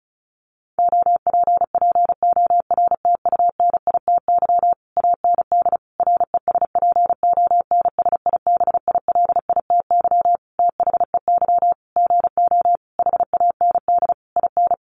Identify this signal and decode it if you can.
Morse code, heard as a keyed tone, about 34 wpm, 700 Hz. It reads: OPPORTUNITY AND RESPONSIBILITY THEY GO HAND IN
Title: Sentences from Top 1200 Words 35wpm